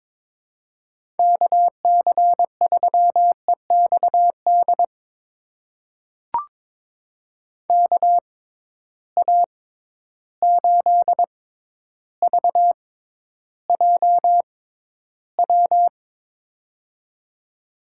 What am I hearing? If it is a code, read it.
Morse code, heard as a keyed tone, about 22 wpm, spaced as for 11 wpm, 700 Hz. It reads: KC3EXD KA8VJW
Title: USA Call Signs - 6 Characters - 6x Character Spacing - 22wpm